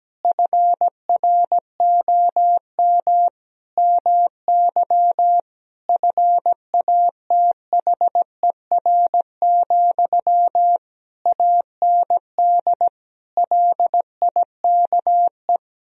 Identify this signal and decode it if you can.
Morse code, heard as a keyed tone, about 17 wpm, 700 Hz. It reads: FROM MY FATHER, AND LIKE
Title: Sentences from Top 1000 Words - Review Part 1 - 17wpm